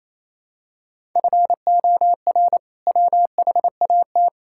Morse code, heard as a keyed tone, about 28 wpm, 700 Hz. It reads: FOR WHAT
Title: Sentences from Top 1000 Words - Review Part 5 - 28wpm